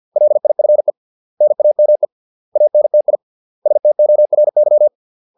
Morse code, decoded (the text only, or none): LIFE NAME ANTI STORY